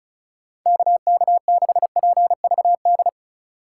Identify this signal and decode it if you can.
Morse code, heard as a keyed tone, about 34 wpm, 700 Hz. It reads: KK6PVD
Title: USA Call Signs - 6 Characters - 3x Character Spacing - 35wpm